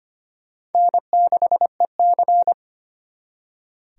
Morse code, heard as a keyed tone, about 31 wpm, 700 Hz. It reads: N6EC
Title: Speed Racing - USA 1x2 Call Signs 25wpm